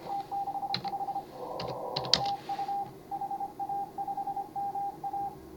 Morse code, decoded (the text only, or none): T65 TU HA5MA